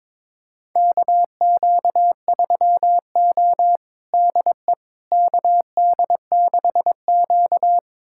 Morse code, heard as a keyed tone, about 22 wpm, 700 Hz. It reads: KQ3O DE KD6Q
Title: USA Callsign DE Callsign - 4 Characters - 22wpm